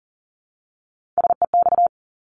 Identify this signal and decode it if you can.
Morse code, heard as a keyed tone, about 40 wpm, 690 Hz. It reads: SEX
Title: Multi-letter Phonemes Repeated 3x 40wpm